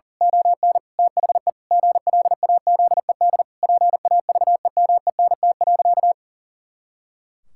Morse code, heard as a keyed tone, about 37 wpm, 700 Hz. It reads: ON THE GLAZED PAVEMENT.
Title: Pangram Challenge v2 - 40wpm